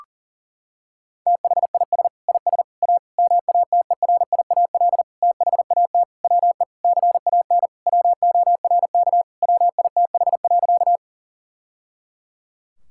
Morse code, heard as a keyed tone, about 40 wpm, 700 Hz. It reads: THIS IS A MATERIAL THAT WE CAN WORK WITH.
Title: Sentences from Top 1000 Words - Review Part 5 - 40wpm